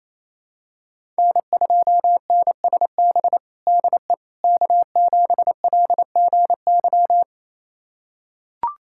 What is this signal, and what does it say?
Morse code, heard as a keyed tone, about 28 wpm, 700 Hz. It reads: N2NSB DE K7LGY
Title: USA Callsign DE Callsign - 5 Characters - 28wpm